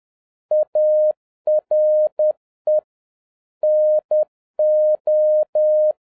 Morse code, heard as a keyed tone, about 10 wpm, 610 Hz. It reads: ARE NO
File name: 2011-07-mp3 / n16 JAPAN TO CANCEL MORSE CODE EXAMINATION 10wpm